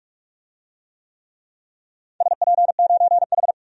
Morse code, heard as a keyed tone, about 44 wpm, 700 Hz. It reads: SP9H
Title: DXCC-100 Call Signs - 4 Characters - 45wpm